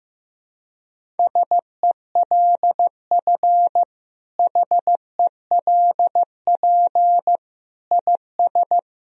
Morse code, heard as a keyed tone, about 15 wpm, 700 Hz. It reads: SELF HELP IS
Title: Sentences from Top 1000 Words - Review Part 1 - 15wpm